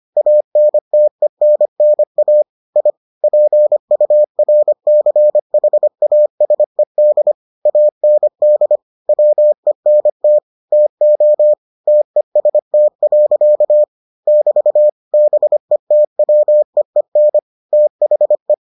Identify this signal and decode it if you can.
Morse code, heard as a keyed tone, about 25 wpm, 600 Hz. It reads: ANTENNA I PURCHASED AND WENT TO TEST. <BT> BETWEEN THE